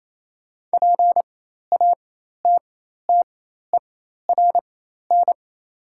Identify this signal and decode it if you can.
Morse code, heard as a keyed tone, about 27 wpm, spaced as for 13 wpm, 700 Hz. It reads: PATTERN
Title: 4x Character Spacing - Top 500 Words - 28wpm